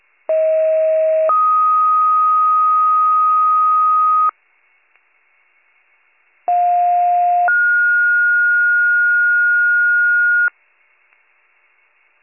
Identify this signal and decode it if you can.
Two-tone page, heard as a one-second tone, then a three-second tone, about 640 then 1250 Hz, about 710 then 1430 Hz. It is heard from 0.3 to 10.5 s.